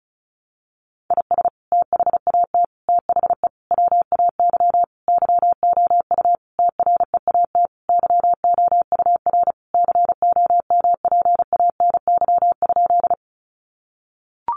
Morse code, heard as a keyed tone, about 34 wpm, 700 Hz. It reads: IS THAT THE WAY YOU TREAT YOUR COMPANY?
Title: Sentences from Top 1300 Words 35wpm